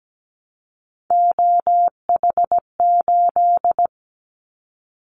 Morse code, heard as a keyed tone, about 17 wpm, 700 Hz. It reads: OH8
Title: Callsign Prefixes for Top 25 DXCC - 17wpm